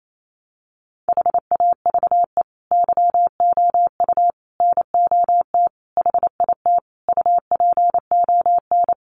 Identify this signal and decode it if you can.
Morse code, heard as a keyed tone, about 28 wpm, 700 Hz. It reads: HAVE YOU NOT HIT UPON
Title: Sentences from Top 1200 Words 28wpm